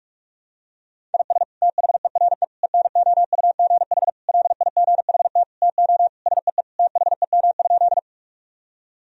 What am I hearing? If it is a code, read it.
Morse code, heard as a keyed tone, about 46 wpm, 700 Hz. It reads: IS THERE ENOUGH LIGHT TO SEE THEM?
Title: Sentences from Top 1000 Words - Review Part 1 - 45wpm